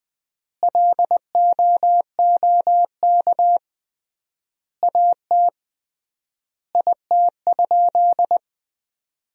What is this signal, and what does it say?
Morse code, heard as a keyed tone, about 20 wpm, 700 Hz. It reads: LOOK AT IT?
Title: 3x Word-Spacing - Sentences from Top 100 Words - 20wpm